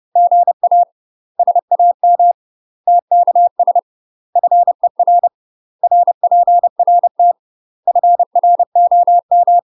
Morse code, heard as a keyed tone, about 30 wpm, 700 Hz. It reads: GA SAM TKS FER RPRT FROM
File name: UK MASTER QSOs TRAINING / 30WPM / UK MASTER QSOs 0046